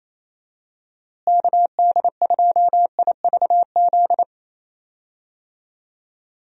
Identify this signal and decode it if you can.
Morse code, heard as a keyed tone, about 27 wpm, 700 Hz. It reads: KD2IVZ
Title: USA Call Signs - 6 Characters - Repeated 2x - 28wpm